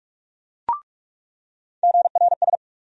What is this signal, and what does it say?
Morse code, heard as a keyed tone, about 46 wpm, 700 Hz. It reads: GRS
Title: USA Call Sign Postfixes - 3 Letters 45wpm